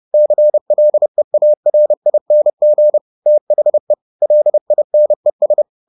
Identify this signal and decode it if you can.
Morse code, heard as a keyed tone, about 30 wpm, 600 Hz. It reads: CLEARING THE LINES
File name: the old man and the sea / the old man and the sea 1 30wpm